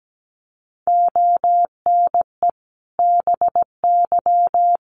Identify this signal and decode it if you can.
Morse code, heard as a keyed tone, about 17 wpm, 700 Hz. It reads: ONE BY
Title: Sentences from Top 1200 Words 17wpm